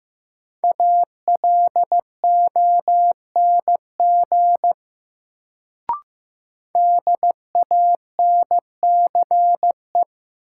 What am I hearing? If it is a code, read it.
Morse code, heard as a keyed tone, about 15 wpm, 700 Hz. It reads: ALONG DANCE
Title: Sentences from Top 700 Words - New Word Drill - 15wpm